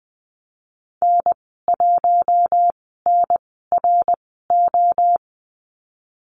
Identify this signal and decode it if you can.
Morse code, heard as a keyed tone, about 20 wpm, 700 Hz. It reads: N 1 N R O
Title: 055 Lesson 10g - Letter r - Call Signs 20wpm